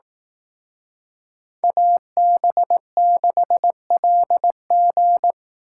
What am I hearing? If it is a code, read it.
Morse code, heard as a keyed tone, about 18 wpm, 700 Hz. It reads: AB6LG
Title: Speed Racing - USA 2x2 Call Signs 15wpm